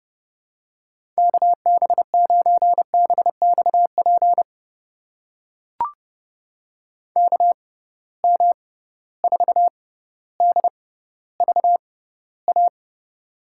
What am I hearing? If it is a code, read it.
Morse code, heard as a keyed tone, about 31 wpm, spaced as for 15 wpm, 700 Hz. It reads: KB9BXP KM4DVA
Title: USA Call Signs - 6 Characters - 6x Character Spacing - 30wpm